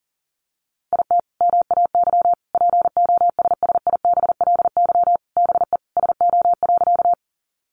Morse code, heard as a keyed tone, about 40 wpm, 700 Hz. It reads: IT MAY POSSIBLY BE SO.
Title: Sentences from Top 1300 Words 40wpm